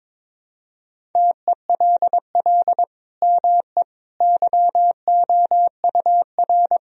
Morse code, heard as a keyed tone, about 22 wpm, 700 Hz. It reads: TELL ME YOUR
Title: Sentences from Top 1000 Words - Review Part 1 - 22wpm